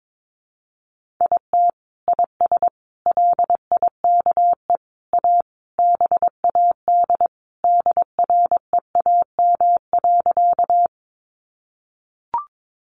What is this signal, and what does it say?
Morse code, heard as a keyed tone, about 22 wpm, 700 Hz. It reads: IT IS LIKE A BAD DREAM.
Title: Sentences from Top 1500 Words 22wpm